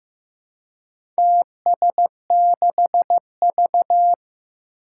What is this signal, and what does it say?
Morse code, heard as a keyed tone, about 15 wpm, 700 Hz. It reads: TS6V
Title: Four Challenging Letter-Number Combinations 15wpm